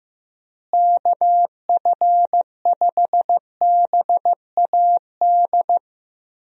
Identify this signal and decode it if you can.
Morse code, heard as a keyed tone, about 15 wpm, 700 Hz. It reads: KF5BAD
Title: USA Call Signs - 6 Characters - Repeated 2x - 15wpm